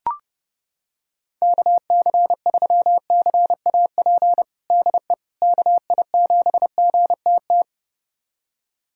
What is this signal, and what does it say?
Morse code, heard as a keyed tone, about 32 wpm, 700 Hz. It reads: KC3CAP DE KI7GTT
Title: USA Callsign DE Callsign 30wpm